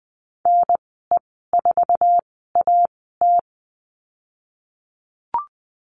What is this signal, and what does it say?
Morse code, heard as a keyed tone, about 19 wpm, spaced as for 14 wpm, 700 Hz. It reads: NE4AT
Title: 055 Lesson 10g - Letter r - Call Signs 20wpm